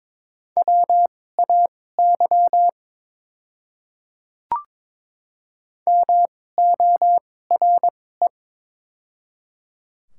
Morse code, heard as a keyed tone, about 22 wpm, spaced as for 16 wpm, 700 Hz. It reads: WAY MORE
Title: 2x Character Spacing - Top 100 Words - 22wpm